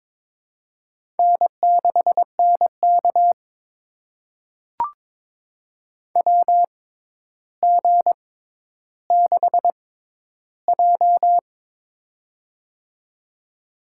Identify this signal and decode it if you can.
Morse code, heard as a keyed tone, about 22 wpm, spaced as for 11 wpm, 700 Hz. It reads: N6NK WG6J
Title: USA Call Signs - 4 Characters - 6x Character Spacing - 22wpm